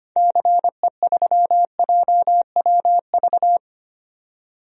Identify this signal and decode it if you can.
Morse code, heard as a keyed tone, about 25 wpm, 700 Hz. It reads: CE3JWV
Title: DXCC-100 Call Signs - 6 Characters - 25wpm